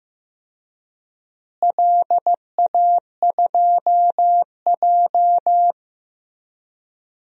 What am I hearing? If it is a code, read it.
Morse code, heard as a keyed tone, about 15 wpm, 700 Hz. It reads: LA2J
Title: DXCC-100 Call Signs - 4 Characters - Repeated 2x - 15wpm